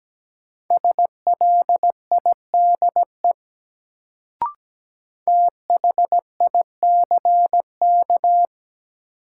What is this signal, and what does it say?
Morse code, heard as a keyed tone, about 17 wpm, 700 Hz. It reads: SLIDE THICK
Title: Sentences from Top 1600 Words - New Word Drill Alt - 17wpm